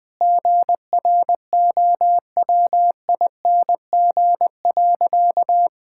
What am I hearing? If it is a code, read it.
Morse code, heard as a keyed tone, about 20 wpm, 700 Hz. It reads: GROWING.
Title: Sentences with 2-Word Phrases - 20wpm